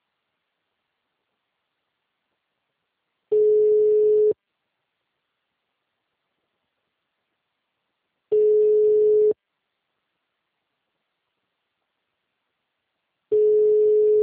Ringback tone (Europe, 425 Hz).
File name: out- -2002- - - .27852